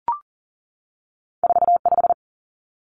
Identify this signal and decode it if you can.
Morse code, heard as a keyed tone, about 40 wpm, 700 Hz. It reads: 45
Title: Numbers - 2-Digit 40wpm